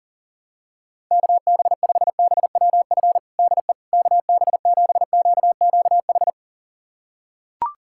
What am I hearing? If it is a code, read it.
Morse code, heard as a keyed tone, about 40 wpm, 700 Hz. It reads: KB5BWF DE KB7QQH
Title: USA Callsign DE Callsign 40wpm